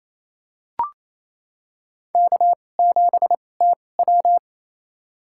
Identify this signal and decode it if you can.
Morse code, heard as a keyed tone, about 28 wpm, spaced as for 20 wpm, 700 Hz. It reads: K7TW
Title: USA Call Signs - 4 Characters - 2x Character Spacing - 28wpm